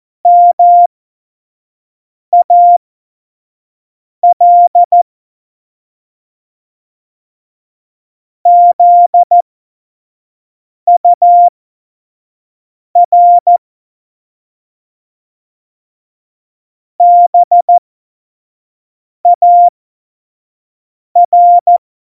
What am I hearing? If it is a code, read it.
Morse code, heard as a keyed tone, about 14 wpm, spaced as for 5 wpm, 700 Hz. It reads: MAL ZUR BAR
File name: Morsekurs / Witze / Witz 12 Mai 2020, 25 F70